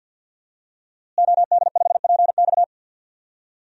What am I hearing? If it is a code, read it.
Morse code, heard as a keyed tone, about 52 wpm, 700 Hz. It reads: OD5PX
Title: DXCC-100 Call Signs - 5 Characters - Repeated 3x - 50wpm